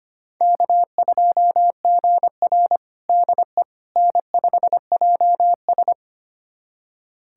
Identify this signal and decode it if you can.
Morse code, heard as a keyed tone, about 25 wpm, 700 Hz. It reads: K2GR DE N5JS
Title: USA Callsign DE Callsign - 4 Characters - 25wpm